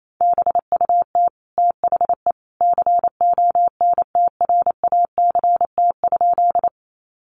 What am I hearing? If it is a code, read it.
Morse code, heard as a keyed tone, about 28 wpm, 700 Hz. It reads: BUT THE CONTRACT?
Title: Sentences from Top 1400 Words 28wpm